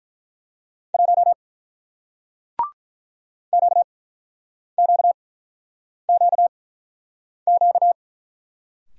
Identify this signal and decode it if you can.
Morse code, heard as a keyed tone, about 48 wpm, spaced as for 9 wpm, 700 Hz. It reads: 1 QQQQ